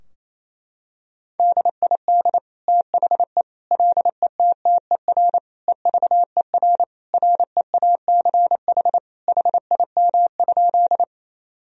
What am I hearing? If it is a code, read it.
Morse code, heard as a keyed tone, about 28 wpm, 700 Hz. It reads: DID THE LETTER EVER REACH HIM?
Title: Sentences from Top 1000 Words - Review Part 2 - 28wpm